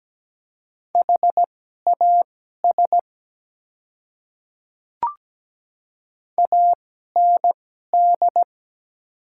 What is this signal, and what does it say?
Morse code, heard as a keyed tone, about 17 wpm, spaced as for 12 wpm, 700 Hz. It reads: HAS AND
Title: 2x Character Spacing - Top 100 Words - 17wpm